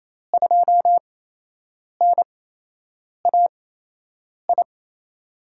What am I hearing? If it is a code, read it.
Morse code, heard as a keyed tone, about 28 wpm, spaced as for 8 wpm, 700 Hz. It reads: 2NAI
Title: USA Call Signs - 6 Characters - 8x Character Spacing - 28wpm